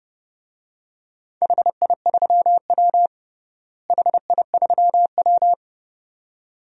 Morse code, heard as a keyed tone, about 30 wpm, 700 Hz. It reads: HI3W HI3W
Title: DXCC-100 Call Signs - 4 Characters - Repeated 2x - 30wpm